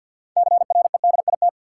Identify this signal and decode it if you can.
Morse code, heard as a keyed tone, about 53 wpm, 700 Hz. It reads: CREDIT
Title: Sentences from Top 1000 Words - New Word Drill - 50wpm